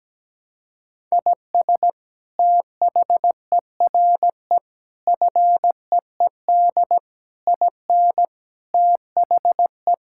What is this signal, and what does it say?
Morse code, heard as a keyed tone, about 17 wpm, 700 Hz. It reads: IS THERE FEED IN THE
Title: Sentences from Top 1600 Words 17wpm